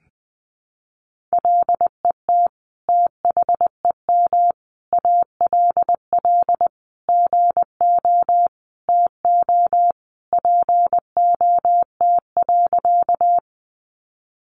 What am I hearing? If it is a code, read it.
Morse code, heard as a keyed tone, about 19 wpm, 700 Hz. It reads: LET THEM ALL GO TO POT.
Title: Sentences from Top 1000 Words 20wpm